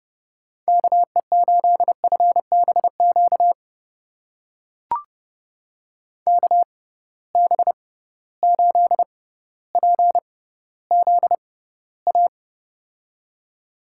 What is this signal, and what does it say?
Morse code, heard as a keyed tone, about 30 wpm, spaced as for 15 wpm, 700 Hz. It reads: KE8FBQ KB8PZA
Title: USA Call Signs - 6 Characters - 6x Character Spacing - 30wpm